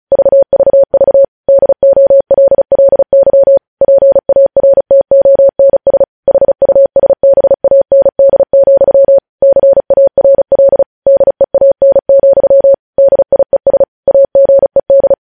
Morse code, heard as a keyed tone, about 36 wpm, 550 Hz. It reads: VVV DOLLY PARTONS HUSBAND, CARL DEAN, DIES AGED
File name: podcast / News headlines-1x-35x35